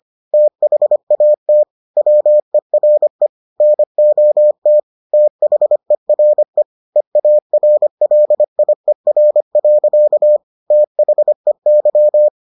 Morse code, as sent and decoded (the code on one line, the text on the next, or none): - .... .- - / .-- . .-. / -. --- - / - .... .-. / . .- .-. .-.. .. .-. .-.-.- / - .... -.--
THAT WERE NOT THERE EARLIER. THEY